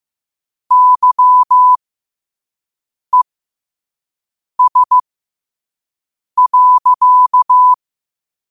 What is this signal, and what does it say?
Morse code, heard as a keyed tone, about 15 wpm, spaced as for 5 wpm, 1000 Hz. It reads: YES.
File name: 5wpm